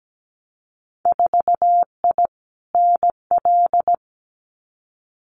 Morse code, heard as a keyed tone, about 17 wpm, 700 Hz. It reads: 4I NL
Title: Winter Field Day Exchanges Encore 17wpm